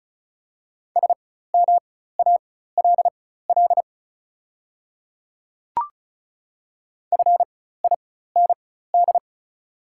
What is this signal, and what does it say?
Morse code, heard as a keyed tone, about 36 wpm, spaced as for 16 wpm, 690 Hz. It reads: SMALL FIND
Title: 4x Character Spacing - Top 300 Words - 35wpm